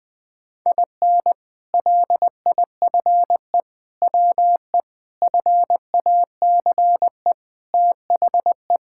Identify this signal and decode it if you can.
Morse code, heard as a keyed tone, about 20 wpm, 700 Hz. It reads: IN LIFE WE FACE THE
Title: Sentences with 7-Word Phrases - 20wpm